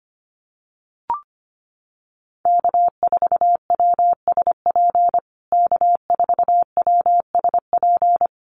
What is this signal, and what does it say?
Morse code, heard as a keyed tone, about 25 wpm, 700 Hz. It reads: K4WSP K4WSP
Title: USA Call Signs Repeated 2x 25wpm